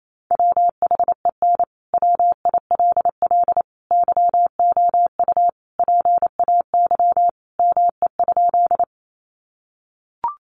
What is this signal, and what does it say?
Morse code, heard as a keyed tone, about 28 wpm, 700 Hz. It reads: WHEN WILL YOU PAY ME?
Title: Sentences from Top 700 Words 28wpm